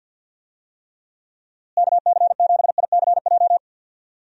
Morse code, heard as a keyed tone, about 50 wpm, 700 Hz. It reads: KC7ICJ
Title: USA Call Signs - 6 Characters - Repeated 3x - 50wpm